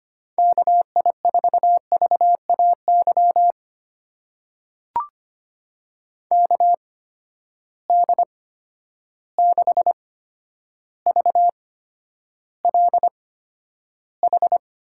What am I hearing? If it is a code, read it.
Morse code, heard as a keyed tone, about 25 wpm, spaced as for 11 wpm, 700 Hz. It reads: KI4VAY KD6VLH
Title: USA Call Signs - 6 Characters - 8x Character Spacing - 25wpm